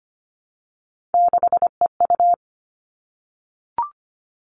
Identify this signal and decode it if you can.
Morse code, heard as a keyed tone, about 25 wpm, 700 Hz. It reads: 6EU
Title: All Cut Numbers - 3-Digit 25wpm